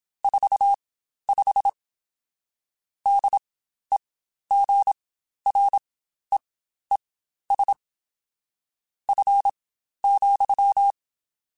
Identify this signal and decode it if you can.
Morse code, heard as a keyed tone, about 27 wpm, spaced as for 12 wpm, 800 Hz. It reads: 45 DEGREES F,
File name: Spaced Characters / Spaced Characters-mp3s / Weather 3 17 sp @25